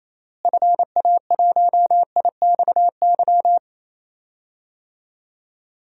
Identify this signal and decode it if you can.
Morse code, heard as a keyed tone, about 28 wpm, 700 Hz. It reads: FA1IXY
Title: DXCC-100 Call Signs - 6 Characters - Repeated 2x - 28wpm